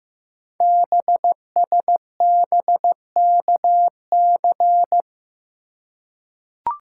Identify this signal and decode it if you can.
Morse code, heard as a keyed tone, about 15 wpm, 700 Hz. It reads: BSBKC
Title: Code Groups - 5 Letters - 15wpm